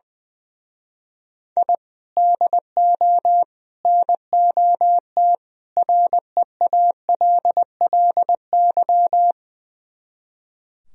Morse code, heard as a keyed tone, about 20 wpm, 700 Hz. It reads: I DO NOT REALLY
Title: Speed-Racing - Sets of 4 Words 20wpm